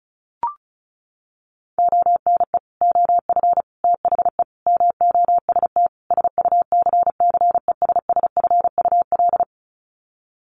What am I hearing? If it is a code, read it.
Morse code, heard as a keyed tone, about 34 wpm, 700 Hz. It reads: ONE OF THE MOST SUCCESSFUL
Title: Sets of 5 Words Encore2 35wpm